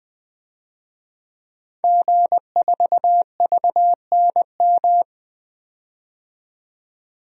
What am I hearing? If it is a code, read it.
Morse code, heard as a keyed tone, about 20 wpm, 700 Hz. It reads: G4VNM
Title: DXCC-100 Call Signs - 5 Characters - 20wpm